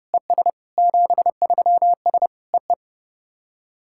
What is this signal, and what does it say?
Morse code, heard as a keyed tone, about 30 wpm, 700 Hz. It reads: ES 73S EE
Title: SOTA QSO Practice 30wpm